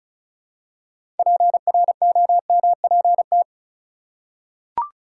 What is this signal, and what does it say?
Morse code, heard as a keyed tone, about 36 wpm, 700 Hz. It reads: PROMPT
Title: Six Letter Words - 35wpm